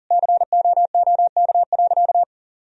Morse code, heard as a keyed tone, about 40 wpm, 700 Hz. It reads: COOK.
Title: Sentences from Top 1000 Words - Review Part 4 - 40wpm